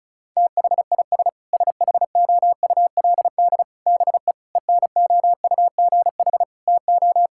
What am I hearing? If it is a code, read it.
Morse code, heard as a keyed tone, about 36 wpm, 700 Hz. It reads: THIS SHOULD BE ENOUGH TO
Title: Sentences from Top 1000 Words - Review Part 6 - 35wpm